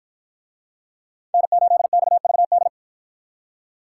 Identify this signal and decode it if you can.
Morse code, heard as a keyed tone, about 53 wpm, 700 Hz. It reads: N8XVD